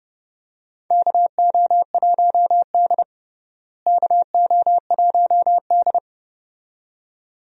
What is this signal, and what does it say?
Morse code, heard as a keyed tone, about 32 wpm, 700 Hz. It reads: KO1D KO1D
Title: USA Call Signs - 4 Characters - Repeated 2x - 30wpm